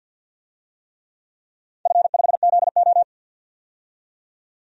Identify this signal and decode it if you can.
Morse code, heard as a keyed tone, about 49 wpm, 700 Hz. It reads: F5ZO
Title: DXCC-100 Call Signs - 4 Characters - 50wpm